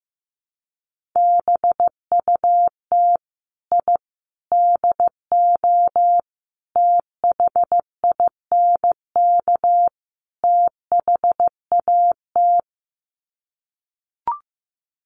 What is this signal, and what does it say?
Morse code, heard as a keyed tone, about 15 wpm, 700 Hz. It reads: BUT I DO THINK THAT
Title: Sets of 5 Words Encore2 15wpm